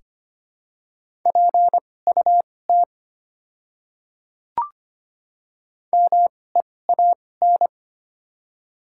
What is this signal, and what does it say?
Morse code, heard as a keyed tone, about 25 wpm, spaced as for 18 wpm, 690 Hz. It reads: PUT MEAN
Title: 2x Character Spacing - Top 200 Words - 25wpm